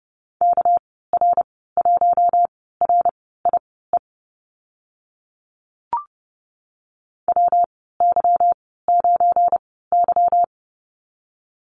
Morse code, heard as a keyed tone, about 30 wpm, spaced as for 17 wpm, 700 Hz. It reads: KR1RIE WY9Y
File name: Lesson 37g - Letter x - Call Signs 30wpm